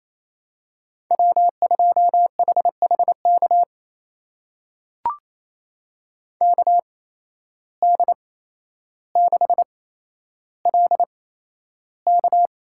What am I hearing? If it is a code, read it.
Morse code, heard as a keyed tone, about 28 wpm, spaced as for 12 wpm, 700 Hz. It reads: W2HHK KD6LK
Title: USA Call Signs - 5 Characters - 8x Character Spacing - 28wpm